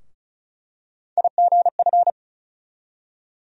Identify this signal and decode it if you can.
Morse code, heard as a keyed tone, about 36 wpm, 700 Hz. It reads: IGF
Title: USA Call Sign Postfixes - 3 Letters 35wpm